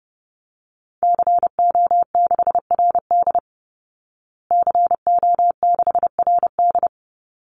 Morse code, heard as a keyed tone, about 30 wpm, 700 Hz. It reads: CO6RD CO6RD
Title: Call Signs Repeated 2x - 30wpm